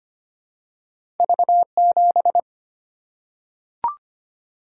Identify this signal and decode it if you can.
Morse code, heard as a keyed tone, about 25 wpm, 700 Hz. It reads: V7
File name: Two Letter-Number Combinations - 25wpm